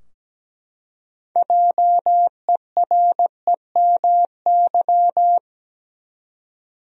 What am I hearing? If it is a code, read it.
Morse code, heard as a keyed tone, about 17 wpm, 700 Hz. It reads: JEREMY
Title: USA Common Names - 17wpm